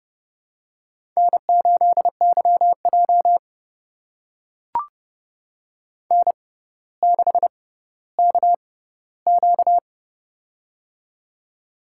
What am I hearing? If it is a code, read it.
Morse code, heard as a keyed tone, about 29 wpm, spaced as for 15 wpm, 700 Hz. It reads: N8YJ N6KQ
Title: USA Call Signs - 4 Characters - 6x Character Spacing - 30wpm